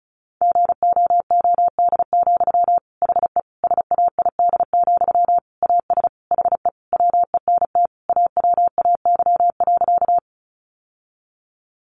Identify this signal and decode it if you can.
Morse code, heard as a keyed tone, about 34 wpm, 700 Hz. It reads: GOOD, HE SAID, AS HE WENT AWAY.
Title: Sentences from Top 600 Words 35wpm